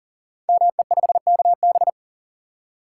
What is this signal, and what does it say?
Morse code, heard as a keyed tone, about 39 wpm, 700 Hz. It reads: ME5KB
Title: Code Groups - 5 Letter:Numbers - 40wpm